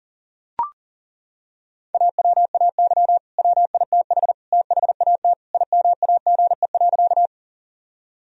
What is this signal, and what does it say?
Morse code, heard as a keyed tone, about 37 wpm, 700 Hz. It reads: AWAY WITH THAT IMAGE.
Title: Sentences from Top 1000 Words - Review Part 4 - 40wpm